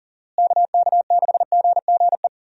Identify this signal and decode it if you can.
Morse code, heard as a keyed tone, about 40 wpm, 700 Hz. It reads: KK6GGE
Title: USA Call Signs - 6 Characters - Repeated 2x - 40wpm